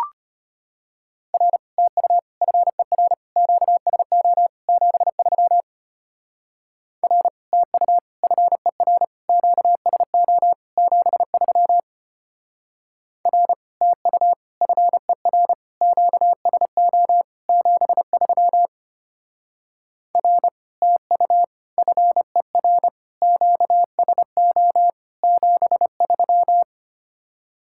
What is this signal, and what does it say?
Morse code, heard as a keyed tone, about 31 wpm, 700 Hz. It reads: R TU FER QSO 73 R TU FER QSO 73 R TU FER QSO 73 R TU FER QSO 73
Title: Speed-Racing Common QSO Phrases - 25wpm